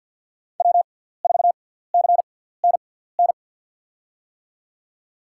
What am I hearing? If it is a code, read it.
Morse code, heard as a keyed tone, about 49 wpm, spaced as for 17 wpm, 690 Hz. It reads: W4CNN
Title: USA Call Signs - 5 Characters - 6x Character Spacing - 50wpm